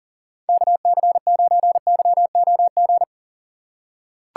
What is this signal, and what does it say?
Morse code, heard as a keyed tone, about 38 wpm, 700 Hz. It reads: KC9YOG
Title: USA Call Signs - 6 Characters - 2x Character Spacing - 40wpm